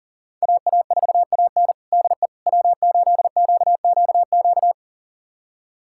Morse code, heard as a keyed tone, about 40 wpm, 700 Hz. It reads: AA4AN DE W8QQQ
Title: USA Callsign DE Callsign 40wpm